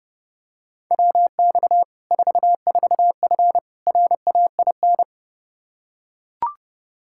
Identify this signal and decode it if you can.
Morse code, heard as a keyed tone, about 30 wpm, 700 Hz. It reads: WX 44F RAIN
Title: Practice QSO - Exchange 2 - Info - 30wpm